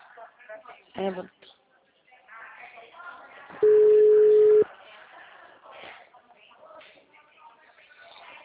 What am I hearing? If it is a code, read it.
Ringback tone (Europe, 425 Hz), with speech.